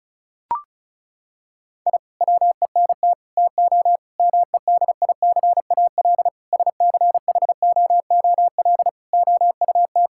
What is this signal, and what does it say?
Morse code, heard as a keyed tone, about 36 wpm, 700 Hz. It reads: I WENT TO MEDICAL SCHOOL OUT
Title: Sentences from Top 1000 Words - Review Part 5 - 35wpm